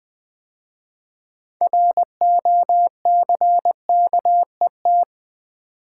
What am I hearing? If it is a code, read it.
Morse code, heard as a keyed tone, about 20 wpm, 700 Hz. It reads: ROCKET
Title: 6 Letter Words - First 3 Letters Given - 20wpm